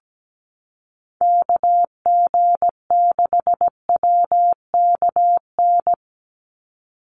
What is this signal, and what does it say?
Morse code, heard as a keyed tone, about 17 wpm, 700 Hz. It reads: KG6WKN
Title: USA Call Signs 17wpm